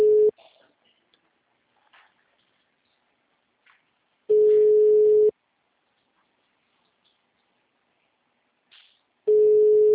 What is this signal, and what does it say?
Ringback tone (Europe, 425 Hz).